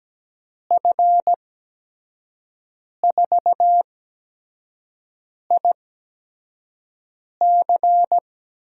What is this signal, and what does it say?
Morse code, heard as a keyed tone, about 17 wpm, spaced as for 5 wpm, 700 Hz. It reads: F4IC
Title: USA Call Signs - 5 Characters - 8x Character Spacing - 17wpm